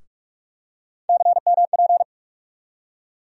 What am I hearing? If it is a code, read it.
Morse code, heard as a keyed tone, about 43 wpm, 700 Hz. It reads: CMP
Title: USA Call Sign Postfixes - 3 Letters 45wpm